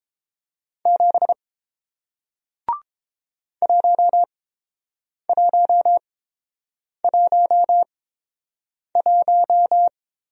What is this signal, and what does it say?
Morse code, heard as a keyed tone, about 31 wpm, spaced as for 7 wpm, 700 Hz. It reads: 7 1111